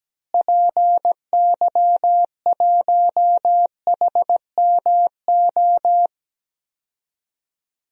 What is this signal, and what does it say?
Morse code, heard as a keyed tone, about 17 wpm, 700 Hz. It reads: PY1HMO
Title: DXCC-100 Call Signs - 6 Characters - Repeated 2x - 17wpm